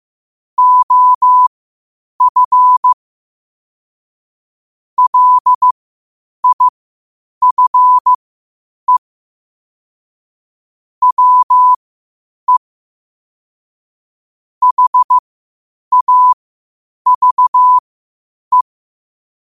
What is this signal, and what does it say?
Morse code, heard as a keyed tone, about 15 wpm, spaced as for 8 wpm, 1000 Hz. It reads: OF LIFE WE HAVE
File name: qotdfiles / 8wpm